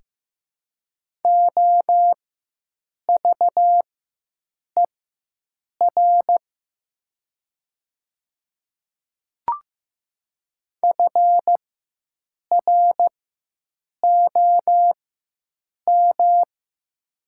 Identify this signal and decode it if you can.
Morse code, heard as a keyed tone, about 14 wpm, spaced as for 7 wpm, 700 Hz. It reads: OVER FROM
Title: 4x Character Spacing - Top 500 Words - 15wpm